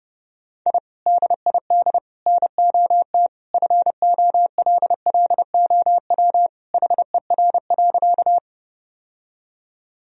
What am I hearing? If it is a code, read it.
Morse code, heard as a keyed tone, about 30 wpm, 700 Hz. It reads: I DID NOT FOLLOW HER.
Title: Sentences from Top 1000 Words - Review Part 1 - 30wpm